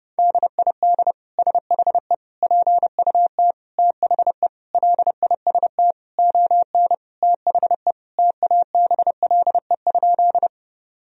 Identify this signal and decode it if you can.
Morse code, heard as a keyed tone, about 32 wpm, 700 Hz. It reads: DID SHE PUT THE LIST ON THE TABLE?
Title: Sentences from Top 1000 Words - Review Part 2 - 30wpm